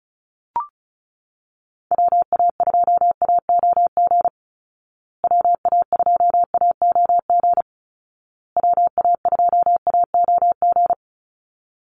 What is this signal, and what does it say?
Morse code, heard as a keyed tone, about 34 wpm, 700 Hz. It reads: WA2AOG WA2AOG WA2AOG
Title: Call Signs Repeated 2x - 35wpm